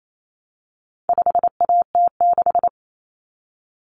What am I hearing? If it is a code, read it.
Morse code, heard as a keyed tone, about 28 wpm, 700 Hz. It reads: 5AT6
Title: Common Cut Numbers - 4-Digit 28wpm